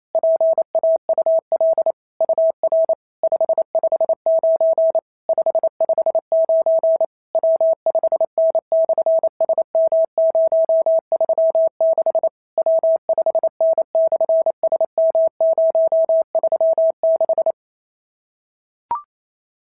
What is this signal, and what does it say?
Morse code, heard as a keyed tone, about 28 wpm, 650 Hz. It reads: PAUL UR 559 559 W5N/SM036 W5N/SM036
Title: SOTA QSO Practice 28wpm